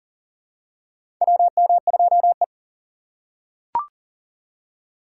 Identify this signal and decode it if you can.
Morse code, heard as a keyed tone, about 40 wpm, 700 Hz. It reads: WM2E
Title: USA Call Signs - 2x Character Spacing - 40wpm